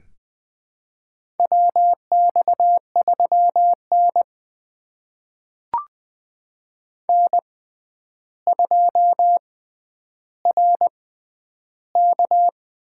Morse code, heard as a keyed tone, about 20 wpm, spaced as for 10 wpm, 700 Hz. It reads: WX3N N2RK
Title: USA Call Signs - 4 Characters - 6x Character Spacing - 20wpm